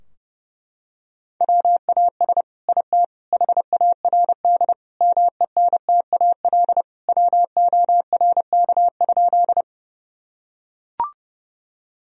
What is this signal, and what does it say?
Morse code, heard as a keyed tone, about 30 wpm, 700 Hz. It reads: WAS IT HARD MENTAL WORK?
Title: Sentences from Top 1000 Words - Review Part 4 - 30wpm